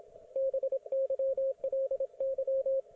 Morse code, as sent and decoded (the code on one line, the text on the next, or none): -... -.-- .-.. -.--
BYLY